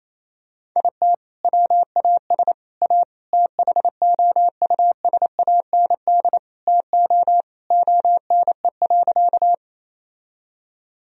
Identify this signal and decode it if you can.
Morse code, heard as a keyed tone, about 28 wpm, 700 Hz. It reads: IT WAS A THOUSAND TO ONE.
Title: Sentences from Top 1000 Words - Review Part 3 - 28wpm